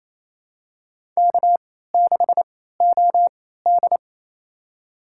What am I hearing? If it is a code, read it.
Morse code, heard as a keyed tone, about 28 wpm, spaced as for 16 wpm, 700 Hz. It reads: K6OD
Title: USA Call Signs - 4 Characters - 3x Character Spacing - 28wpm